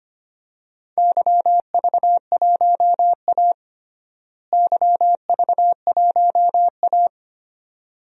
Morse code, heard as a keyed tone, about 25 wpm, 700 Hz. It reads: YV1A YV1A
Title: DXCC-100 Call Signs - 4 Characters - Repeated 2x - 25wpm